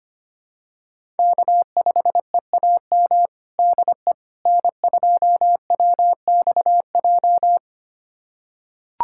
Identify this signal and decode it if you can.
Morse code, heard as a keyed tone, about 25 wpm, 700 Hz. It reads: K5EAM DE N2WXJ
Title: USA Callsign DE Callsign - 5 Characters - 25wpm